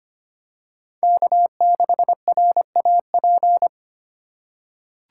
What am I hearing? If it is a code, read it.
Morse code, heard as a keyed tone, about 25 wpm, 700 Hz. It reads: K6RAP
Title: USA Call Signs - 5 Characters - 2x Character Spacing - 25wpm